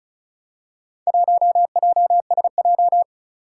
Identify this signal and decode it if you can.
Morse code, heard as a keyed tone, about 36 wpm, 700 Hz. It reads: 1JSJ
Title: Four Challenging Letter-Number Combinations 35wpm